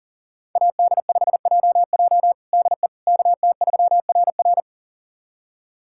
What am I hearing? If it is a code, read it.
Morse code, heard as a keyed tone, about 40 wpm, 700 Hz. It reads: AD5JJ DE KT3RR
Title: USA Callsign DE Callsign - 5 Characters - 40wpm